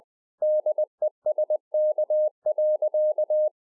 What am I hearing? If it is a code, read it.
Morse code, heard as a keyed tone, about 20 wpm, 610 Hz. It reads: DESK.